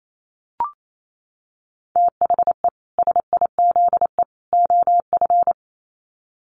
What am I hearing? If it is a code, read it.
Morse code, heard as a keyed tone, about 28 wpm, 700 Hz. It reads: THE SIZE OF
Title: Sets of 3 Words 28wpm